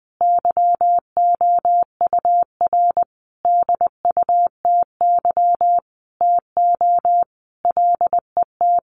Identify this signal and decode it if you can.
Morse code, heard as a keyed tone, about 20 wpm, 700 Hz. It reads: YOUR DUTY TO LET
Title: Sentences from Top 1500 Words 20wpm